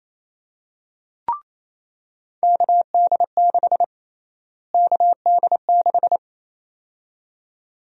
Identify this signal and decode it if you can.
Morse code, heard as a keyed tone, about 28 wpm, 700 Hz. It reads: KD6 KD6
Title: USA Call Sign Prefixes - Repeated 2x - 28wpm